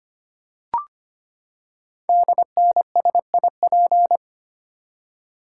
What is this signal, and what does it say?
Morse code, heard as a keyed tone, about 25 wpm, 700 Hz. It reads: DNSIP
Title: Code Groups - 5 Letters - 25wpm